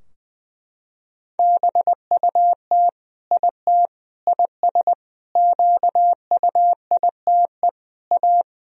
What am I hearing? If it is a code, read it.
Morse code, heard as a keyed tone, about 20 wpm, 700 Hz. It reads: BUT IT IS QUITE A
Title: Sentences from Top 1000 Words - Review Part 2 - 20wpm